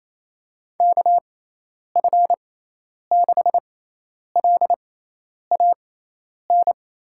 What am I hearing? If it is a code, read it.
Morse code, heard as a keyed tone, about 27 wpm, spaced as for 10 wpm, 700 Hz. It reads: KF6LAN
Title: USA Call Signs - 6 Characters - 6x Character Spacing - 28wpm